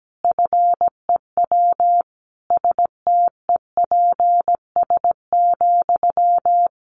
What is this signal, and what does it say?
Morse code, heard as a keyed tone, about 17 wpm, 700 Hz. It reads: FEW STEPS,
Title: A Sound Of Thunder 17wpm